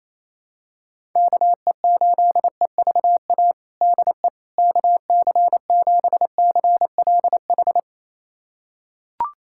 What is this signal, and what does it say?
Morse code, heard as a keyed tone, about 28 wpm, 700 Hz. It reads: KE8EVA DE KC7CLH
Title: USA Callsign DE Callsign 28wpm